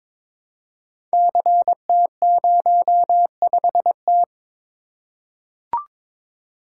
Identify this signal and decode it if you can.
Morse code, heard as a keyed tone, about 22 wpm, 700 Hz. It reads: CT05T
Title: Code Groups - 5 Letter:Numbers - 22wpm